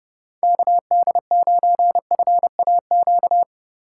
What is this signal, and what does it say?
Morse code, heard as a keyed tone, about 29 wpm, 700 Hz. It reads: KD9FAQ